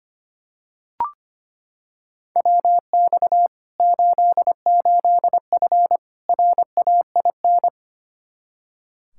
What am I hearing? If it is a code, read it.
Morse code, heard as a keyed tone, about 25 wpm, 700 Hz. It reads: WX 88F RAIN
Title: Practice QSO - Exchange 2 - Info - 25wpm